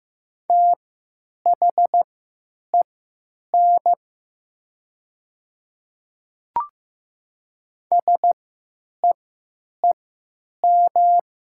Morse code, heard as a keyed tone, about 15 wpm, spaced as for 9 wpm, 700 Hz. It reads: THEN SEEM
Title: 3x Character Spacing - Top 500 Words - 15wpm